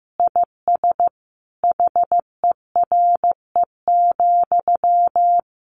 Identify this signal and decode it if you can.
Morse code, heard as a keyed tone, about 15 wpm, 700 Hz. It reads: IS HERE,
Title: Sentences from Top 1000 Words 15wpm